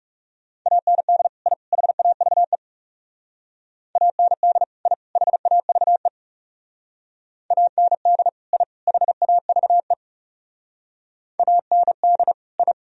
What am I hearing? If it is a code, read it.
Morse code, heard as a keyed tone, about 35 wpm, 700 Hz. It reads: AND I HAVE AND I HAVE AND I HAVE AND I
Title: Speed-Racing - Sets of 3 Words 30wpm